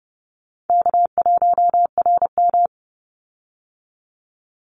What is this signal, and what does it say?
Morse code, heard as a keyed tone, about 32 wpm, 700 Hz. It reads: K1RM
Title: Easy Call Signs - 30wpm